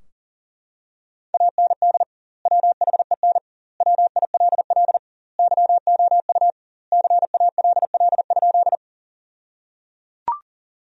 Morse code, heard as a keyed tone, about 37 wpm, 700 Hz. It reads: AND WHEN WILL YOU CALL?
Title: 2x Word-Spacing - Sentences from Top 100 Words - 40wpm